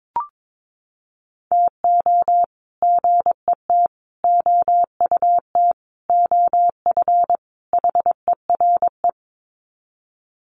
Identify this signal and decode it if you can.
Morse code, heard as a keyed tone, about 22 wpm, 700 Hz. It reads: TO GET OUT OF HERE
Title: Sets of 5 Words Encore 22wpm